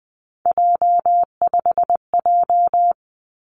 Morse code, heard as a keyed tone, about 20 wpm, 700 Hz. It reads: J5J
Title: Three Challenging Letter-Number Combinations 20wpm